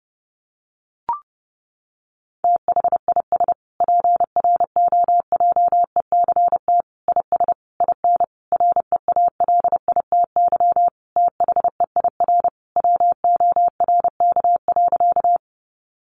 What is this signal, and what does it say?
Morse code, heard as a keyed tone, about 30 wpm, 700 Hz. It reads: THIS PROJECT IS IN REALITY THEIR WORK.
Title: Sentences from Top 1400 Words 30wpm